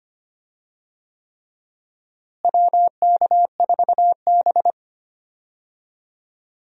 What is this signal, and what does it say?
Morse code, heard as a keyed tone, about 25 wpm, 700 Hz. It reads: WK4B
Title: USA Call Signs - 4 Characters - Repeated 3x - 25wpm